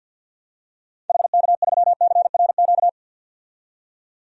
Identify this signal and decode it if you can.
Morse code, heard as a keyed tone, about 48 wpm, 700 Hz. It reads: HK3CRQ
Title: DXCC-100 Call Signs - 6 Characters - 50wpm